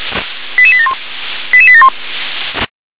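Five-tone selective call (ZVEI), heard as quick runs of five tones, tones of about 70 ms. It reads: 88871 88871
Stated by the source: ZVEI five-tone selective call received by SDR.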